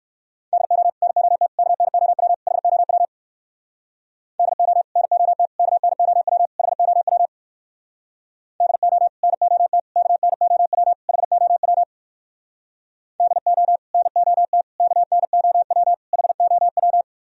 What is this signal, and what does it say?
Morse code, heard as a keyed tone, about 59 wpm, 700 Hz. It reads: DO NOT KNOW HOW DO NOT KNOW HOW DO NOT KNOW HOW DO NOT KNOW HOW